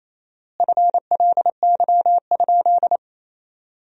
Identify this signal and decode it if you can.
Morse code, heard as a keyed tone, about 28 wpm, 700 Hz. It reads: FLY?
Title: 4x Word-Spacing - Sentences from Top 500 Words - 28wpm